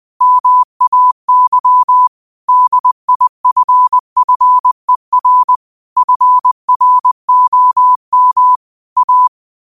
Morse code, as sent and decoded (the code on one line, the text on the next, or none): -- .- -.-- / -.. .. ..-. ..-. .-. / ..-. .-. --- -- / .-
MAY DIFFER FROM A